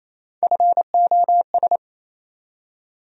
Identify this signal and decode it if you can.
Morse code, heard as a keyed tone, about 28 wpm, 700 Hz. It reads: FOS
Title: USA Call Sign Postfixes - 3 Letters 28wpm